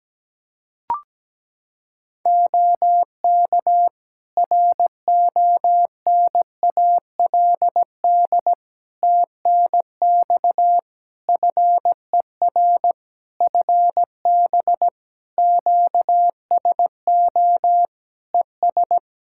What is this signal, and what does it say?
Morse code, heard as a keyed tone, about 17 wpm, 700 Hz. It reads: OK RONALD TNX FER FB QSO ES
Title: Practice QSO - Exchange 4 - Ending - 17wpm